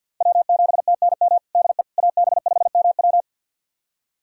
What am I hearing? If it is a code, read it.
Morse code, heard as a keyed tone, about 49 wpm, 700 Hz. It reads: W7TNM DE AB5MW
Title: USA Callsign DE Callsign - 5 Characters - 50wpm